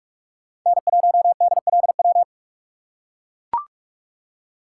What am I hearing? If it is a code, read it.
Morse code, heard as a keyed tone, about 43 wpm, 700 Hz. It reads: N1DLW